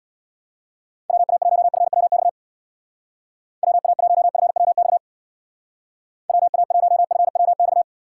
Morse code, heard as a keyed tone, about 61 wpm, 700 Hz. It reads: WA1FWX WA1FWX WA1FWX
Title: Speed Racing - USA 2x3 Call Signs 50wpm